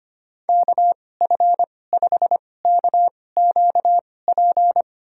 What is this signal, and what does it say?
Morse code, heard as a keyed tone, about 25 wpm, spaced as for 18 wpm, 700 Hz. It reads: KF5KQP
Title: USA Call Signs - 6 Characters - 2x Character Spacing - 25wpm